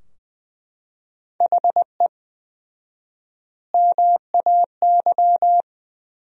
Morse code, heard as a keyed tone, about 21 wpm, 700 Hz. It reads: HE MAY
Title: 4x Word-Spacing - Sentences from Top 300 Words - 20wpm